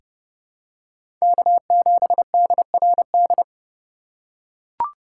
Morse code, heard as a keyed tone, about 32 wpm, 700 Hz. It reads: K7DRD
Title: USA Call Signs - 4x Character Spacing - 30wpm